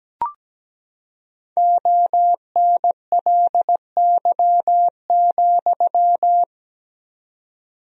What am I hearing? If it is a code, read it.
Morse code, heard as a keyed tone, about 17 wpm, 700 Hz. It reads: ONLY,
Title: 4x Word-Spacing - Sentences from Top 300 Words - 17wpm